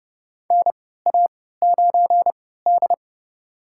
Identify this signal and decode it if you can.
Morse code, heard as a keyed tone, about 32 wpm, spaced as for 17 wpm, 700 Hz. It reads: NA9D
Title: USA Call Signs - 4 Characters - 3x Character Spacing - 30wpm